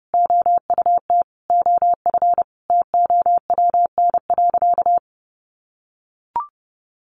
Morse code, heard as a keyed tone, about 28 wpm, 700 Hz. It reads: OUT OF TOWN.
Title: Sentences from Top 1300 Words 30wpm